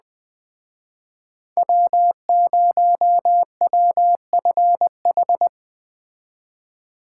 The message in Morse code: W0WFH

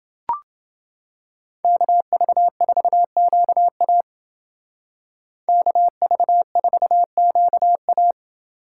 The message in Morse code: KV4QA KV4QA

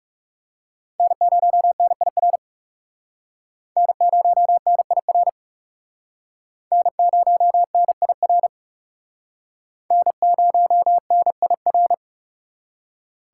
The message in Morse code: N0NIR N0NIR N0NIR N0NIR

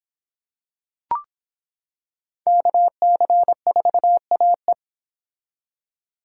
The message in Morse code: KC4AE

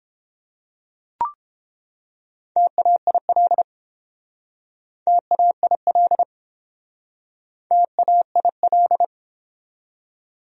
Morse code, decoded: TAIL TAIL TAIL

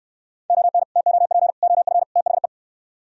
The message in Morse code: ON TOP OF THE

